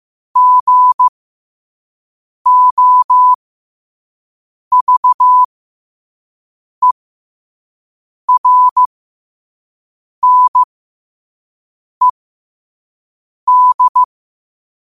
Morse code, heard as a keyed tone, about 15 wpm, spaced as for 5 wpm, 1000 Hz. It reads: GOVERNED
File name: qotdfiles / 5wpm